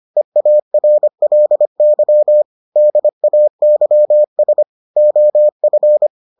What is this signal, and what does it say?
Morse code, heard as a keyed tone, about 25 wpm, 600 Hz. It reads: EARLY DAYS OF